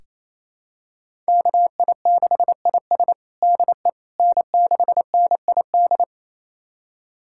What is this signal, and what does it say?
Morse code, heard as a keyed tone, about 28 wpm, 700 Hz. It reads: KI6IS DE N6NID
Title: USA Callsign DE Callsign - 5 Characters - 28wpm